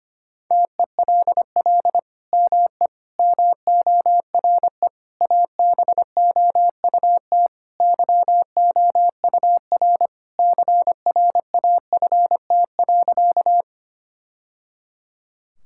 Morse code, heard as a keyed tone, about 25 wpm, 700 Hz. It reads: TELL ME MORE ABOUT YOUR CRAFT.
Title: Sentences from Top 1000 Words - Review Part 6 - 25wpm